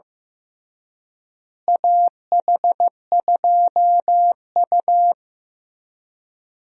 Morse code, heard as a keyed tone, about 15 wpm, 700 Hz. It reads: AH2U
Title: USA Call Signs - 4 Characters - Repeated 3x - 15wpm